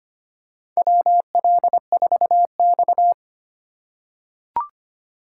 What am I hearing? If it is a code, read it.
Morse code, heard as a keyed tone, about 25 wpm, 700 Hz. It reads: WL4X
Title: USA Call Signs - 4 Characters - 3x Character Spacing - 25wpm